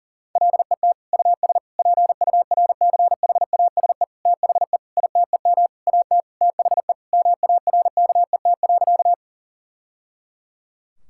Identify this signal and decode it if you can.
Morse code, heard as a keyed tone, about 40 wpm, 700 Hz. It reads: LET US PURCHASE THE ITEM AT THE MARKET.
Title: Sentences from Top 1000 Words - Review Part 6 - 40wpm